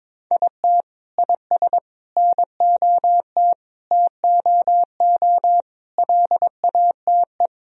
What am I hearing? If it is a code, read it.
Morse code, heard as a keyed tone, about 22 wpm, 700 Hz. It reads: IT IS NOT TOO LATE
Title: Sentences from Top 1000 Words - Review Part 2 - 22wpm